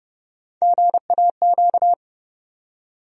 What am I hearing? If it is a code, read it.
Morse code, heard as a keyed tone, about 30 wpm, 700 Hz. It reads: GAQ
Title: USA Call Sign Postfixes - 3 Letters 30wpm